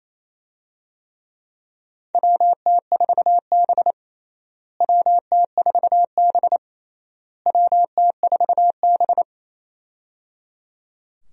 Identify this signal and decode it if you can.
Morse code, heard as a keyed tone, about 28 wpm, 700 Hz. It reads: WT4B WT4B WT4B
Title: USA Call Signs - 4 Characters - Repeated 3x - 28wpm